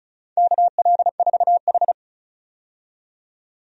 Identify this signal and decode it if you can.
Morse code, heard as a keyed tone, about 36 wpm, 700 Hz. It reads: KL4H
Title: USA Call Signs - 4 Characters - 35wpm